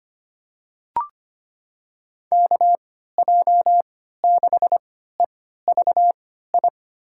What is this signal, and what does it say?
Morse code, heard as a keyed tone, about 25 wpm, spaced as for 14 wpm, 700 Hz. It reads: KJ6EVI
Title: USA Call Signs - 6 Characters - 3x Character Spacing - 25wpm